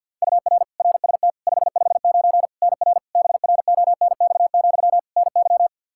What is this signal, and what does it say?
Morse code, heard as a keyed tone, about 50 wpm, 700 Hz. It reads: UR RST 559 NR BRONX, NY
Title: K1USN SST QSO Practice 50wpm